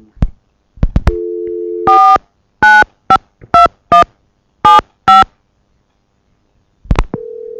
DTMF, heard as telephone touch tones, tones of about 150 ms. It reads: 4952175
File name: Tape038 Side2 All Trey-Mark-Phreak-Out 2005-0912